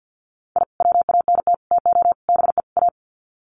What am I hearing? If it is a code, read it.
Morse code, heard as a keyed tone, about 50 wpm, 700 Hz. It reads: I WANT TO BE A
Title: Sets of 5 Words Encore 50wpm